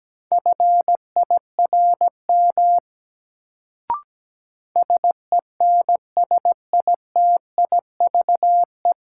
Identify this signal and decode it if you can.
Morse code, heard as a keyed tone, about 17 wpm, 700 Hz. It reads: FIRM SENSITIVE
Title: Sentences from Top 1400 Words - New Word Drill - 17wpm